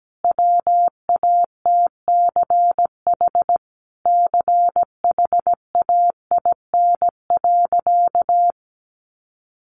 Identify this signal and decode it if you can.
Morse code, heard as a keyed tone, about 17 wpm, 700 Hz. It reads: WATCH CHAIN.
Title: Sentences from Top 1500 Words 17wpm